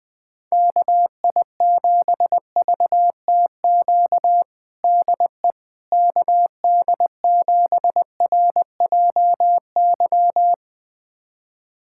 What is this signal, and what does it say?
Morse code, heard as a keyed tone, about 20 wpm, 700 Hz. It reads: KI7VTQ DE KD7RJY
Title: USA Callsign DE Callsign - 6 Characters - 20wpm